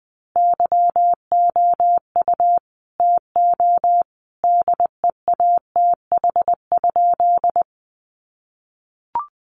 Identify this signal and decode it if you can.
Morse code, heard as a keyed tone, about 20 wpm, 700 Hz. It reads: YOU TO DEATH?
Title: Sentences from Top 1100 Words 20wpm